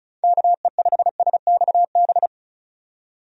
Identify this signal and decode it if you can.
Morse code, heard as a keyed tone, about 36 wpm, 700 Hz. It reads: KE5SXB
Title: USA Call Signs - 6 Characters - 35wpm